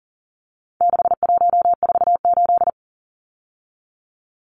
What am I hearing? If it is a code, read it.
Morse code, heard as a keyed tone, about 40 wpm, 700 Hz. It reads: 6148
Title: Numbers - 4-Digit 40wpm